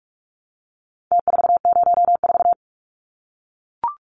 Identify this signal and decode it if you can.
Morse code, heard as a keyed tone, about 44 wpm, 700 Hz. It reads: T404
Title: Numbers - Contest Style - 4-Digit 45wpm